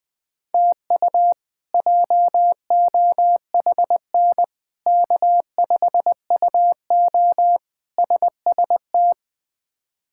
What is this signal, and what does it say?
Morse code, heard as a keyed tone, about 21 wpm, 700 Hz. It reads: TU JOHN K5UO SST
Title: K1USN SST QSO Practice - 20wpm